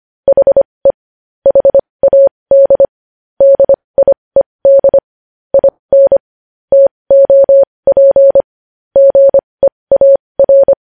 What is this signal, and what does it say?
Morse code, heard as a keyed tone, about 25 wpm, spaced as for 20 wpm, 550 Hz. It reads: HE HAD DIED IN TOP GEAR
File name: podcast / News headlines-1x-25x20